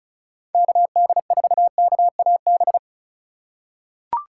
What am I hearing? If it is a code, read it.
Morse code, heard as a keyed tone, about 36 wpm, 700 Hz. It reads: KD4KAB
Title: USA Call Signs - 4x Character Spacing - 35wpm